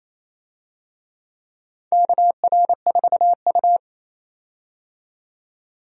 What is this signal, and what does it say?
Morse code, heard as a keyed tone, about 27 wpm, 690 Hz. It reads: KR4U
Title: USA Call Signs - 4 Characters - 28wpm